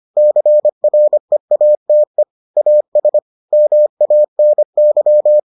Morse code, heard as a keyed tone, about 25 wpm, 600 Hz. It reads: CREATE AS MANY